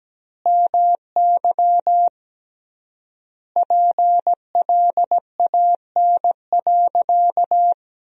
Morse code, heard as a keyed tone, about 17 wpm, 700 Hz. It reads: MY PLAN.
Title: 3x Word-Spacing - Sentences from Top 500 Words - 17wpm